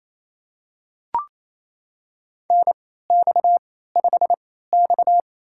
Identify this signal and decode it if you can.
Morse code, heard as a keyed tone, about 28 wpm, spaced as for 16 wpm, 700 Hz. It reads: NX5X
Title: USA Call Signs - 4 Characters - 3x Character Spacing - 28wpm